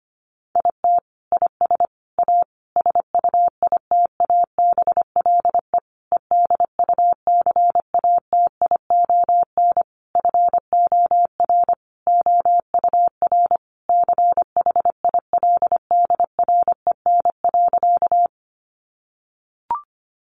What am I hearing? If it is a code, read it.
Morse code, heard as a keyed tone, about 25 wpm, 700 Hz. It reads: IT IS A SUITABLE EDUCATION FOR OUR CHILDREN.
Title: Sentences from Top 1400 Words 25wpm